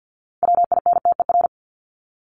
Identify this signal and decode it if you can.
Morse code, heard as a keyed tone, about 48 wpm, 700 Hz. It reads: WINTER
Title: Top 1501-1750 Words 50wpm